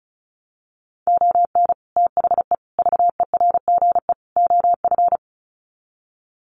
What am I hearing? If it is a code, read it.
Morse code, heard as a keyed tone, about 34 wpm, 700 Hz. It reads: ON THE VERGE OF
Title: Sets of 4 Words 35wpm